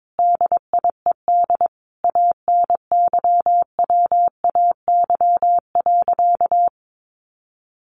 Morse code, heard as a keyed tone, about 22 wpm, 700 Hz. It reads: DIED ANYWAY.
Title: A Sound Of Thunder 22wpm